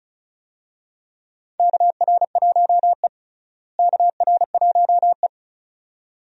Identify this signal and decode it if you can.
Morse code, heard as a keyed tone, about 36 wpm, 700 Hz. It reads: KR1E KR1E
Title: USA Call Signs - 4 Characters - Repeated 2x - 35wpm